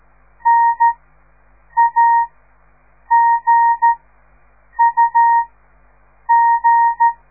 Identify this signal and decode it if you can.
Morse code, heard as a keyed tone, about 13 wpm, spaced as for 7 wpm, 930 Hz. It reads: NAGUG
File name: radiomorningstar / morse / les 4